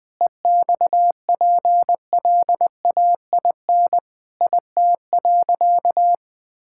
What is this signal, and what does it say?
Morse code, heard as a keyed tone, about 20 wpm, 700 Hz. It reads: EXPLAIN IT.
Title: Sentences with 8-Word Phrases - 20wpm